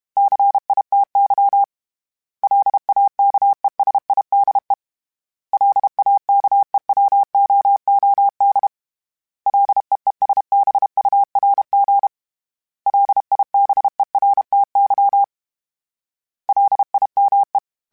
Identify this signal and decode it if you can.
Morse code, heard as a keyed tone, about 32 wpm, 800 Hz. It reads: CITY LAKESIDE LAKEWOOD LEESBURG LIBERTY LIME
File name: fromSMs / FromSMs mp3s / Cities@30